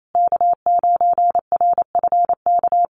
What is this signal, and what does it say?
Morse code, heard as a keyed tone, about 28 wpm, 700 Hz. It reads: K9RFK
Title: USA Call Signs Repeated 3x 28wpm